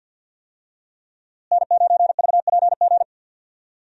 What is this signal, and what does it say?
Morse code, heard as a keyed tone, about 46 wpm, 700 Hz. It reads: N9VPG
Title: USA Call Signs - 5 Characters - Repeated 2x - 50wpm